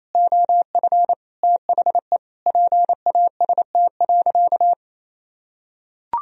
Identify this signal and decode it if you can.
Morse code, heard as a keyed tone, about 28 wpm, 700 Hz. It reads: OF THE PAST.
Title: Sentences from Top 1000 Words - Review Part 3 - 28wpm